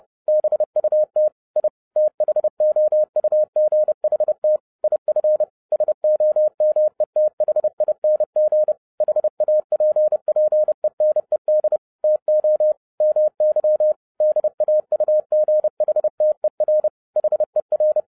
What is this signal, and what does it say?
Morse code, heard as a keyed tone, about 30 wpm, 610 Hz. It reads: BUT I THOUGHT IF SOMETHING HAPPENED TO MY DAUGHTER HER